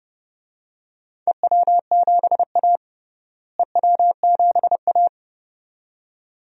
Morse code, heard as a keyed tone, about 29 wpm, 700 Hz. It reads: EW7A EW7A
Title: DXCC-100 Call Signs - 4 Characters - Repeated 2x - 30wpm